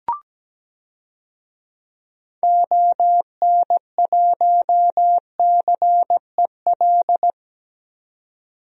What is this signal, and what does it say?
Morse code, heard as a keyed tone, about 17 wpm, 700 Hz. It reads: ON1CEL
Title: DXCC-100 Call Signs - 6 Characters - 17wpm